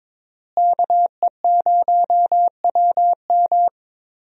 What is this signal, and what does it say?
Morse code, heard as a keyed tone, about 22 wpm, 700 Hz. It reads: KE0WM
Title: USA Call Signs - 5 Characters - Repeated 3x - 22wpm